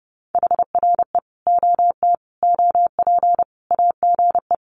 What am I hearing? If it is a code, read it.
Morse code, heard as a keyed tone, about 30 wpm, 700 Hz. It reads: HRE OT OP AGE
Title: Common QSO Phrases 30wpm